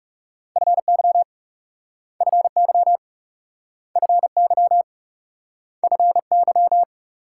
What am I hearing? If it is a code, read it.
Morse code, heard as a keyed tone, about 41 wpm, 700 Hz. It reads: FY FY FY FY